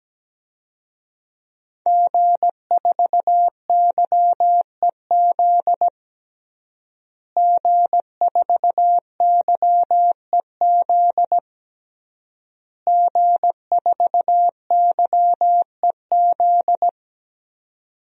Morse code, heard as a keyed tone, about 17 wpm, 700 Hz. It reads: G4YEZ G4YEZ G4YEZ
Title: DXCC-100 Call Signs - 5 Characters - Repeated 3x - 17wpm